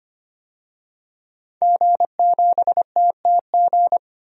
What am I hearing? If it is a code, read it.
Morse code, heard as a keyed tone, about 25 wpm, 700 Hz. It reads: G7TTG
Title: DXCC-100 Call Signs - 5 Characters - 25wpm